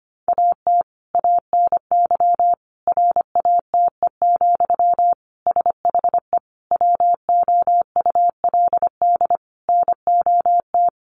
Morse code, heard as a keyed tone, about 25 wpm, 700 Hz. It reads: AT ANY RATE, SHE WOULD NOT
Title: Sentences from Top 1300 Words 25wpm